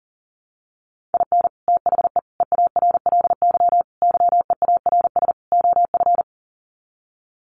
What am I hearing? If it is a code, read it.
Morse code, heard as a keyed tone, about 40 wpm, 700 Hz. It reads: IN THE EARLY YEARS OF
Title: Sets of 5 Words Encore 40wpm